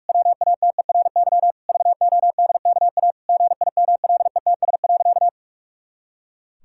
Morse code, heard as a keyed tone, about 46 wpm, 700 Hz. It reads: WATERY VODKA GIMLETS.
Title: Pangram Challenge v2 - 45wpm